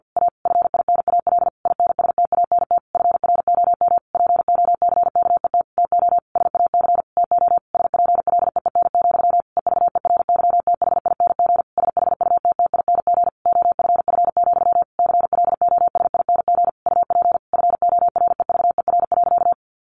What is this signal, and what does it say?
Morse code, heard as a keyed tone, about 50 wpm, 700 Hz. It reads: A FINAL INSTANT FROM POCKET TO SAC TO SPLEEN, EVERYTHING SHUTTING OFF, CLOSING UP FOREVER.
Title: A Sound Of Thunder 50wpm